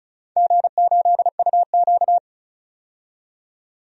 Morse code, heard as a keyed tone, about 36 wpm, 690 Hz. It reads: G8UQ
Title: DXCC-100 Call Signs - 4 Characters - 35wpm